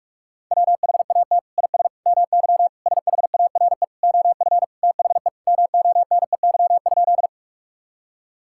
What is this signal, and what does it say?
Morse code, heard as a keyed tone, about 41 wpm, 700 Hz. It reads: WHAT IS MY SHARE OF THE MONEY?
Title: Sentences from Top 1000 Words - Review Part 6 - 45wpm